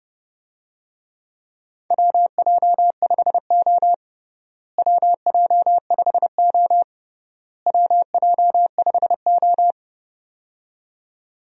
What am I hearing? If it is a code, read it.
Morse code, heard as a keyed tone, about 32 wpm, 700 Hz. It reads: WJ5O WJ5O WJ5O
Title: USA Call Signs - 4 Characters - Repeated 3x - 30wpm